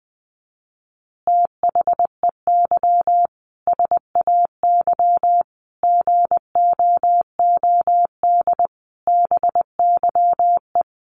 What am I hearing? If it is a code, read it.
Morse code, heard as a keyed tone, about 20 wpm, 700 Hz. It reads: THEY SAY GOOD BYE
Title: Easy Phrases 20wpm